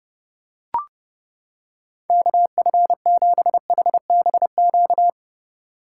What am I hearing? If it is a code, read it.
Morse code, heard as a keyed tone, about 30 wpm, 700 Hz. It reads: KF7HBQ